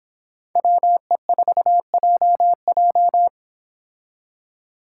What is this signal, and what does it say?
Morse code, heard as a keyed tone, about 29 wpm, 700 Hz. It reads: WE4JJ